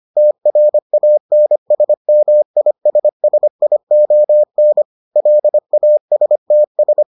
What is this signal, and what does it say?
Morse code, heard as a keyed tone, about 25 wpm, 600 Hz. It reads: TRANSMISSION LASTS